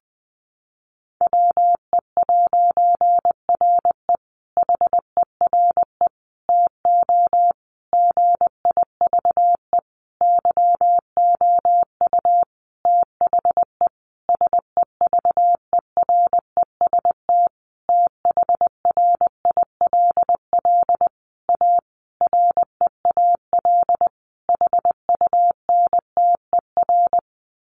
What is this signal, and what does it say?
Morse code, heard as a keyed tone, about 20 wpm, 700 Hz. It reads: WE'RE HERE TO GIVE YOU THE SEVEREST THRILL A REAL HUNTER
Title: A Sound Of Thunder 20wpm